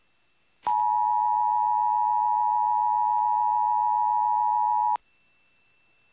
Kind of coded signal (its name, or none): Emergency Alert System (SAME) tones